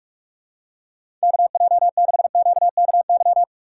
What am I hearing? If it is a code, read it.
Morse code, heard as a keyed tone, about 45 wpm, 700 Hz. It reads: KJ6QKY
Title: USA Call Signs - 6 Characters - 3x Character Spacing - 45wpm